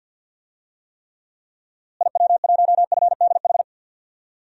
Morse code, heard as a keyed tone, about 52 wpm, 700 Hz. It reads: IW1FDH
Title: DXCC-100 Call Signs - 6 Characters - Repeated 3x - 50wpm